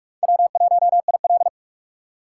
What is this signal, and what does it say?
Morse code, heard as a keyed tone, about 55 wpm, 700 Hz. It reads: W1IL
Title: Speed Racing - USA 1x2 Call Signs 45wpm